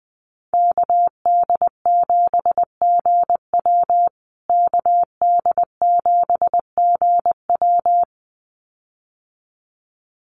Morse code, heard as a keyed tone, about 20 wpm, 700 Hz. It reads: KD7GW KD7GW
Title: USA Call Signs Repeated 2x 20wpm